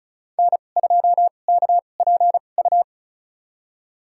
Morse code, heard as a keyed tone, about 36 wpm, spaced as for 25 wpm, 700 Hz. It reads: N2KPU
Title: USA Call Signs - 5 Characters - 2x Character Spacing - 35wpm